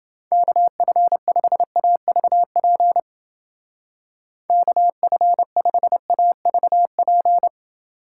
Morse code, heard as a keyed tone, about 29 wpm, 700 Hz. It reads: KF5AVP KF5AVP